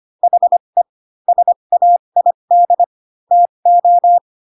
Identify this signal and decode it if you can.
Morse code, heard as a keyed tone, about 25 wpm, spaced as for 22 wpm, 700 Hz. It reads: HE SAID TO
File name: Pinocchio / Pinocchio ch3 18 25 ik1hge